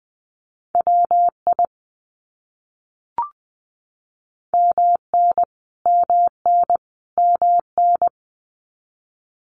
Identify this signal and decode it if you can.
Morse code, heard as a keyed tone, about 20 wpm, 700 Hz. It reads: WI MN MN MN
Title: US States Repeated 3x 20wpm